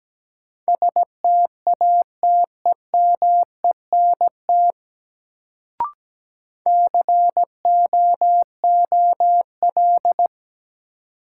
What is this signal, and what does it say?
Morse code, heard as a keyed tone, about 17 wpm, 700 Hz. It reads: STATEMENT COOL
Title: Sentences from Top 1200 Words - New Word Drill Alt - 17wpm